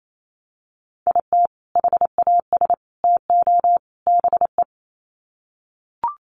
Morse code, heard as a keyed tone, about 28 wpm, 700 Hz. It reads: IT HAS TO BE
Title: Sets of 4 Words 28wpm